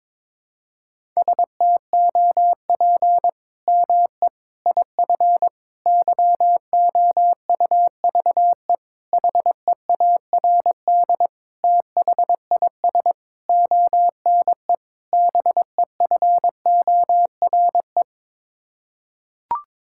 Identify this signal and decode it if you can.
Morse code, heard as a keyed tone, about 22 wpm, 700 Hz. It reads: STOP ME IF YOUVE HEARD THIS ONE BEFORE
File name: practice / 22wpm / Sets of 8 Words - 22wpm